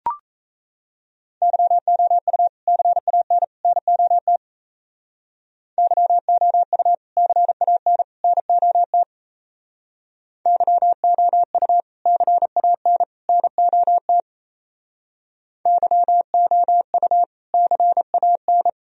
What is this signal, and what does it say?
Morse code, heard as a keyed tone, about 36 wpm, 700 Hz. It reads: YOU CAN NOT YOU CAN NOT YOU CAN NOT YOU CAN